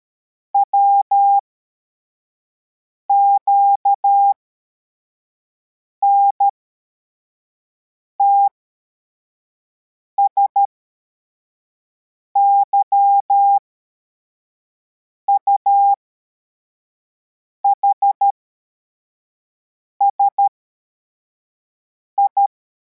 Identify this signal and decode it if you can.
Morse code, heard as a keyed tone, about 13 wpm, spaced as for 4 wpm, 800 Hz. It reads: WQNTS YUHSI
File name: formation / cw / Text71